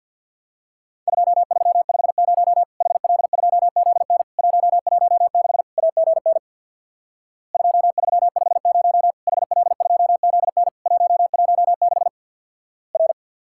Morse code, heard as a keyed tone, about 50 wpm, 700 Hz. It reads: 2350 HL2ZN 116 AGN 2350 HL2ZN 116 R
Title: Worked All Europe DX Contest - QTC Exchange with Repeats - 50wpm